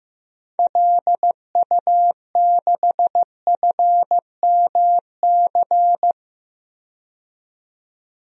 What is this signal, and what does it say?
Morse code, heard as a keyed tone, about 15 wpm, 700 Hz. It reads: LU6FMC
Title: DXCC-100 Call Signs - 6 Characters - 15wpm